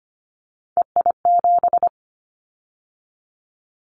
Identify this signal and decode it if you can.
Morse code, heard as a keyed tone, about 25 wpm, 700 Hz. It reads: EI7
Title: Callsign Prefixes for Top 50 DXCC - 25wpm